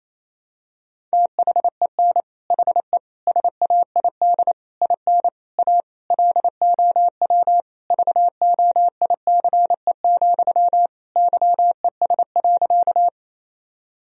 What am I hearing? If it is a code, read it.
Morse code, heard as a keyed tone, about 28 wpm, 700 Hz. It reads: THEN HE SAID IN A LOW VOICE, YES.
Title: Sentences from Top 1000 Words - Review Part 1 - 28wpm